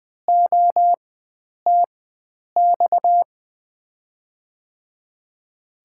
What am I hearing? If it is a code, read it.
Morse code, heard as a keyed tone, about 20 wpm, spaced as for 9 wpm, 700 Hz. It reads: OTX
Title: USA Call Signs - 6 Characters - 4x Character Spacing - 20wpm